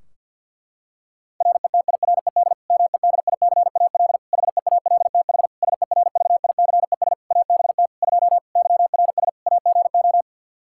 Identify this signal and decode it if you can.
Morse code, heard as a keyed tone, about 52 wpm, 700 Hz. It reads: RETIRED MEDICAL HEALTH SERVICES ABT 2 YRS AGO
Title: Practice QSO - Exchange 3 - More Info - 50wpm